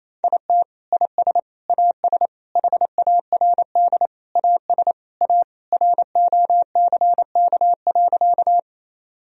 Morse code, heard as a keyed tone, about 28 wpm, 700 Hz. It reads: IT IS AS HARD AS A ROCK.
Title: Sentences from Top 1000 Words - Review Part 2 - 28wpm